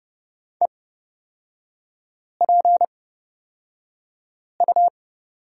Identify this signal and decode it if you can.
Morse code, heard as a keyed tone, about 30 wpm, spaced as for 5 wpm, 700 Hz. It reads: EPU
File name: Lesson 36e - BT - Three Characters 30wpm